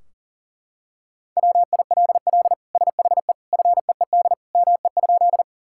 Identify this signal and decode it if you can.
Morse code, heard as a keyed tone, about 40 wpm, 700 Hz. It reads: WILL SHE FEED ME?
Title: Sentences from Top 1600 Words 40wpm